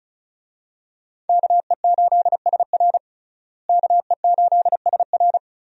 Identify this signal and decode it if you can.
Morse code, heard as a keyed tone, about 36 wpm, 700 Hz. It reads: KE8SR KE8SR
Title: USA Call Signs - 5 Characters - Repeated 3x - 35wpm